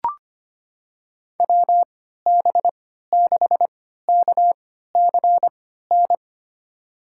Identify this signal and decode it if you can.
Morse code, heard as a keyed tone, about 25 wpm, spaced as for 14 wpm, 700 Hz. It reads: WB6KCN
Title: USA Call Signs - 6 Characters - 3x Character Spacing - 25wpm